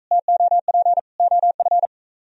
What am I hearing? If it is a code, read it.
Morse code, heard as a keyed tone, about 44 wpm, 700 Hz. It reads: TOP OF